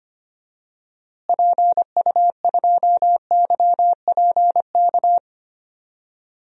Morse code, heard as a keyed tone, about 25 wpm, 700 Hz. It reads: PU2YPK
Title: DXCC-100 Call Signs - 6 Characters - Repeated 2x - 25wpm